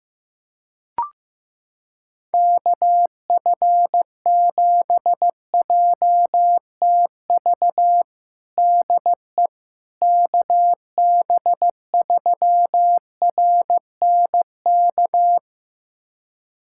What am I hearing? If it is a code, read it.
Morse code, heard as a keyed tone, about 16 wpm, 700 Hz. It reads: KF7JTV DE KB3RNK
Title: USA Callsign DE Callsign 15wpm